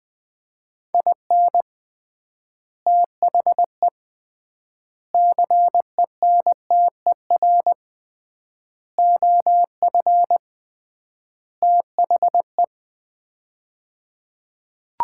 Morse code, heard as a keyed tone, about 20 wpm, 700 Hz. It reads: IN THE CENTER OF THE
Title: 3x Word-Spacing - Sets of 5 Words - 20wpm